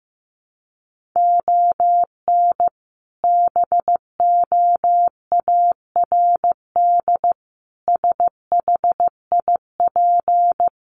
Morse code, heard as a keyed tone, about 15 wpm, 700 Hz. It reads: ON BOARD SHIP
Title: Sentences from Top 1400 Words 15wpm